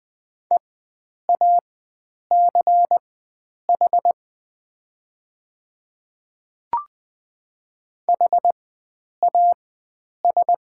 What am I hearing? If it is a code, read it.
Morse code, heard as a keyed tone, about 20 wpm, spaced as for 9 wpm, 700 Hz. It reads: EACH HAS
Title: 4x Character Spacing - Top 100 Words - 20wpm